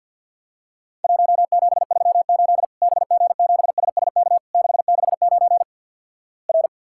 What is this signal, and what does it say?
Morse code, heard as a keyed tone, about 54 wpm, 700 Hz. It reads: 1738 BG7SSK 669 R
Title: Worked All Europe DX Contest - QTC Exchange - 50wpm